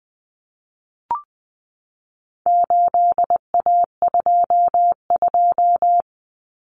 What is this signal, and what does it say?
Morse code, heard as a keyed tone, about 20 wpm, 700 Hz. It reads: 8A22
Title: Common Cut Numbers - 4-Digit 20wpm